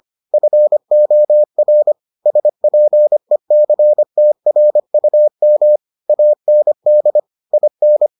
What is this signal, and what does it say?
Morse code, heard as a keyed tone, about 25 wpm, 600 Hz. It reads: FOR SPECTRUM AND IN